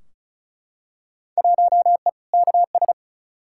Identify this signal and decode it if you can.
Morse code, heard as a keyed tone, about 34 wpm, 700 Hz. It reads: 1E KS
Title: Field Day Exchanges - 35wpm